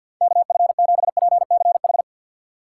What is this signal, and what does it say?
Morse code, heard as a keyed tone, about 50 wpm, 700 Hz. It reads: KF7PCH
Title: USA Call Signs - 2x Character Spacing - 50wpm